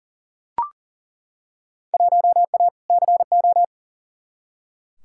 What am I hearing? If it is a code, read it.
Morse code, heard as a keyed tone, about 40 wpm, 700 Hz. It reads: 1A CO
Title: Field Day Exchanges - 40wpm